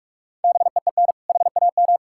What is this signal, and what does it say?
Morse code, heard as a keyed tone, about 41 wpm, 700 Hz. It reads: BEEN HAM
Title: Practice QSO - Exchange 3 - More Info - 45wpm